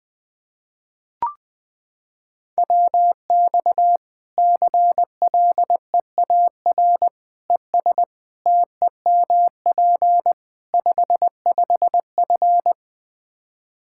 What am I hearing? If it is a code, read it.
Morse code, heard as a keyed tone, about 20 wpm, 700 Hz. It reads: WX CLEAR ES TEMP 55F
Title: Practice QSO - Exchange 2 - Info - 20wpm